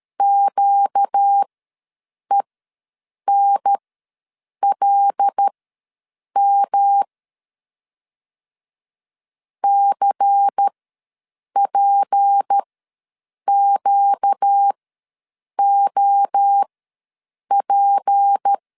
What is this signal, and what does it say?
Morse code, heard as a keyed tone, about 13 wpm, spaced as for 7 wpm, 800 Hz. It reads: QENLM CPQOP
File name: LEZIONI / 6 PKX0 / 6 P ws 12wpm 3117